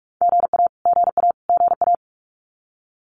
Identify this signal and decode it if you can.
Morse code, heard as a keyed tone, about 44 wpm, 700 Hz. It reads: GA GA GA
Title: Abbreviations Repeated 3x - 45wpm